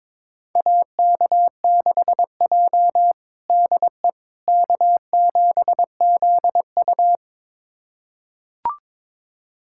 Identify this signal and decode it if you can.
Morse code, heard as a keyed tone, about 22 wpm, 700 Hz. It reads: AK6J DE K7ZU
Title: USA Callsign DE Callsign - 4 Characters - 22wpm